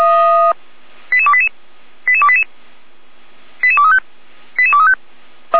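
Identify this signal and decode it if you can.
Five-tone selective call (ZVEI), heard as quick runs of five tones, tones of about 70 ms. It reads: 88288 88288 88235 88235
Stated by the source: ZVEI five-tone selective call received by SDR, from the fire brigade at feuerwehr-orlamuende.org.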